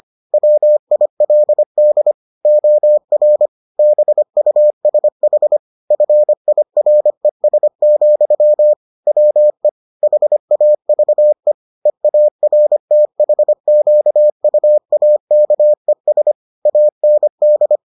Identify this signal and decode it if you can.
Morse code, heard as a keyed tone, about 25 wpm, 600 Hz. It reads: WILD OR BUSH FIRES, WE HAVE EARTHQUAKES AND